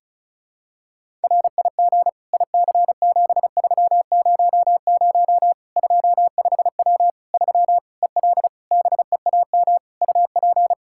Morse code, heard as a keyed tone, about 34 wpm, 700 Hz. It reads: RIG IC7300 25W 3 EL BEAM UP
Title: Practice QSO - Exchange 2 - Info - 35wpm